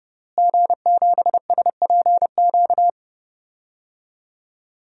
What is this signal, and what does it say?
Morse code, heard as a keyed tone, about 30 wpm, 700 Hz. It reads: G7SPQ
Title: DXCC-100 Call Signs - 5 Characters - Repeated 2x - 30wpm